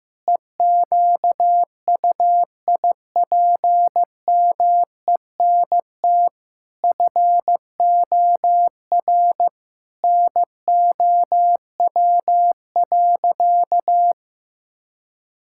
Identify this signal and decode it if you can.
Morse code, heard as a keyed tone, about 15 wpm, 700 Hz. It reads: EQUIPMENT FOR NOW.
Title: Sentences from Top 1000 Words - Review Part 5 - 15wpm